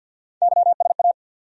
Morse code, heard as a keyed tone, about 46 wpm, 700 Hz. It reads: YSA
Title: USA Call Sign Postfixes - 3 Letters 50wpm